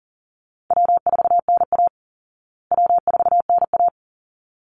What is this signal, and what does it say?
Morse code, heard as a keyed tone, about 40 wpm, 700 Hz. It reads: W4NA W4NA
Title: Easy Call Signs Repeated 2x - 40wpm